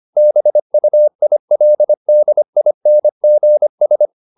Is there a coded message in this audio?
Yes: Morse code from 0.2 to 4.1 s, heard as a keyed tone, about 25 wpm, 600 Hz. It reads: BUILDINGS